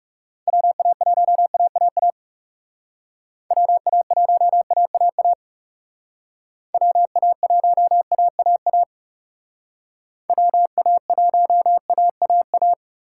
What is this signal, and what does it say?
Morse code, heard as a keyed tone, about 38 wpm, 700 Hz. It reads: WA1AAA WA1AAA WA1AAA WA1AAA